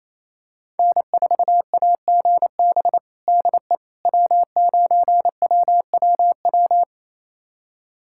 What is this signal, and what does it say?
Morse code, heard as a keyed tone, about 28 wpm, 700 Hz. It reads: N4AGB DE W9WWW
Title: USA Callsign DE Callsign - 5 Characters - 28wpm